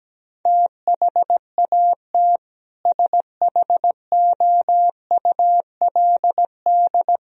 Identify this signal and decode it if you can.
Morse code, heard as a keyed tone, about 17 wpm, 700 Hz. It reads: THAT SHOULD
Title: Sentences from Top 1000 Words - Review Part 2 - 17wpm